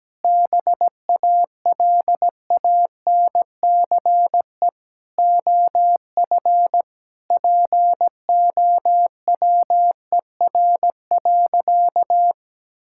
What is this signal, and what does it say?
Morse code, heard as a keyed tone, about 17 wpm, 700 Hz. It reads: BALANCE OF POWER.
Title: Sentences from Top 1000 Words - Review Part 6 - 17wpm